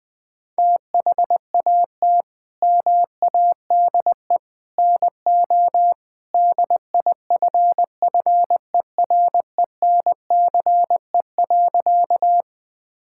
Morse code, heard as a keyed tone, about 20 wpm, 700 Hz. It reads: THAT MADE NO DIFFERENCE.
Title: Sentences from Top 1000 Words - Review Part 5 - 20wpm